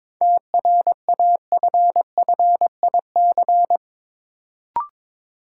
Morse code, heard as a keyed tone, about 22 wpm, 700 Hz. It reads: TRAFFIC
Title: Sentences from Top 1300 Words - New Word Drill Alt - 22wpm